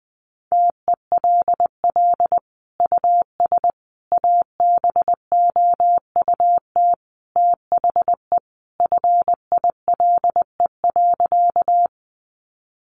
Morse code, heard as a keyed tone, about 20 wpm, 700 Hz. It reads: TELL US ABOUT THE FILE.
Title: Sentences from Top 1200 Words 20wpm